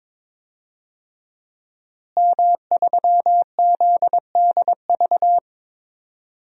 Morse code, heard as a keyed tone, about 23 wpm, 700 Hz. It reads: M3ZDV
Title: DXCC-100 Call Signs - 5 Characters - Repeated 3x - 22wpm